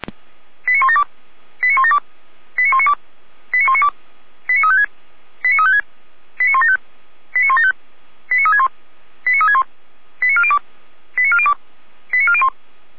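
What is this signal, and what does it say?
Five-tone selective call (ZVEI), heard as quick runs of five tones, tones of about 80 ms. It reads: 89172 89172 89192 89192 89356 89356 89175 89175 89361 89361 89402 89402 89401
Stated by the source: ZVEI five-tone selective call received via SDR.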